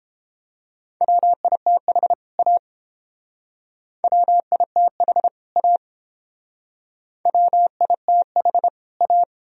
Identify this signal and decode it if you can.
Morse code, heard as a keyed tone, about 31 wpm, 700 Hz. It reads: WITH A WITH A WITH A